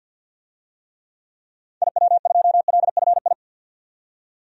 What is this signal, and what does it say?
Morse code, heard as a keyed tone, about 50 wpm, 700 Hz. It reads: IW2LFI
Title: DXCC-100 Call Signs - 6 Characters - 50wpm